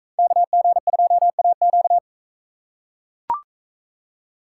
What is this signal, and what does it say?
Morse code, heard as a keyed tone, about 42 wpm, 700 Hz. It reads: KG2AQ